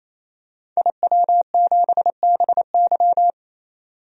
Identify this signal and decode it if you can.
Morse code, heard as a keyed tone, about 28 wpm, 700 Hz. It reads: IW7BY
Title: DXCC-100 Call Signs - 5 Characters - 28wpm